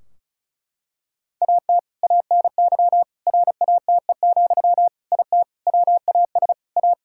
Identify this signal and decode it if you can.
Morse code, heard as a keyed tone, about 36 wpm, 700 Hz. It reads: AT ANY RATE, IT WAS A
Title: Sentences from Top 1600 Words 35wpm